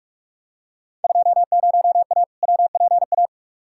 Morse code, heard as a keyed tone, about 45 wpm, 700 Hz. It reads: 20A WPA
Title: Field Day Exchanges Encore - 45wpm